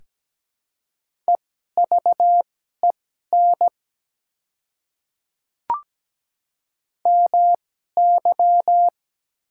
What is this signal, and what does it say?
Morse code, heard as a keyed tone, about 17 wpm, spaced as for 12 wpm, 700 Hz. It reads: EVEN MY
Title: 2x Character Spacing - Top 200 Words - 17wpm